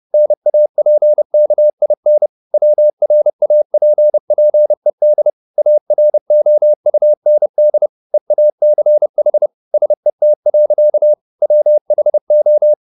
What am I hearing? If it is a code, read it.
Morse code, heard as a keyed tone, about 30 wpm, 600 Hz. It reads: NAPKIN WRAPPED AROUND EACH SET. WHO
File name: the old man and the sea / the old man and the sea 1 30wpm